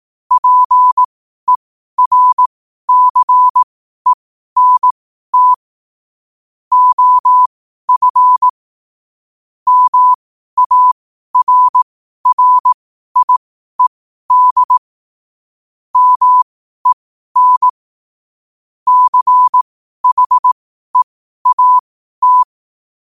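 Morse code, heard as a keyed tone, about 18 wpm, spaced as for 12 wpm, 1000 Hz. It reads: PERCENT OF MARRIED MEN CHEAT
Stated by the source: Morse code quote practice at 12 words per minute.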